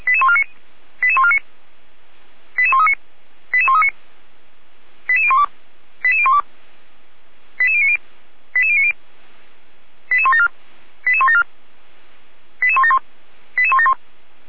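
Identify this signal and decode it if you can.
Five-tone selective call (ZVEI), heard as quick runs of five tones, tones of about 70 ms. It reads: 88149 88139 88139 88012 88012 88090 88090 88174 88174 88171 88171
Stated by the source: ZVEI five-tone selective call received by SDR, from the feuerwehr-orlamuende fire brigade audio archive.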